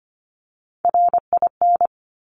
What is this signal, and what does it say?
Morse code, heard as a keyed tone, about 25 wpm, 700 Hz. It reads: RIN
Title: Top Three Letter Combinations in Words 25wpm